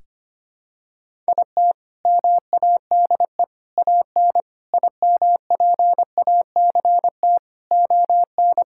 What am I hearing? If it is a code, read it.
Morse code, heard as a keyed tone, about 25 wpm, 700 Hz. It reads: IT MADE AN IMPACT ON
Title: Sentences from Top 1000 Words - Review Part 6 - 25wpm